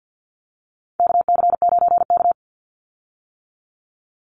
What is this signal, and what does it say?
Morse code, heard as a keyed tone, about 50 wpm, 700 Hz. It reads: KC9K
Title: Call Signs - 50wpm